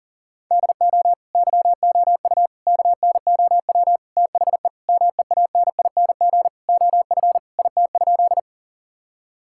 Morse code, heard as a keyed tone, about 40 wpm, 700 Hz. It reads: DO YOU KNOW THE MEANING OF IT?
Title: Sentences from Top 1000 Words - Review Part 6 - 40wpm